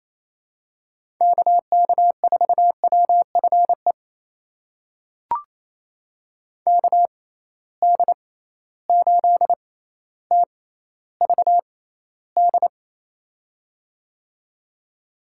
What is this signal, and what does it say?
Morse code, heard as a keyed tone, about 28 wpm, spaced as for 14 wpm, 700 Hz. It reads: KK4WFE KD8TVD
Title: USA Call Signs - 6 Characters - 6x Character Spacing - 28wpm